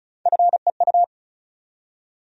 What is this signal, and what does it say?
Morse code, heard as a keyed tone, about 34 wpm, 700 Hz. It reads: FEU
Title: USA Call Sign Postfixes - 3 Letters 35wpm